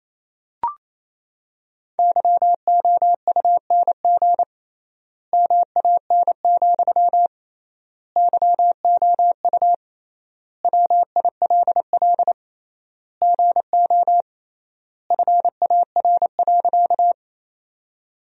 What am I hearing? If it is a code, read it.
Morse code, heard as a keyed tone, about 28 wpm, 700 Hz. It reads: YOUNG MAN, YOU WILL GO FAR.
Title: 3x Word-Spacing - Sentences from Top 400 Words - 28wpm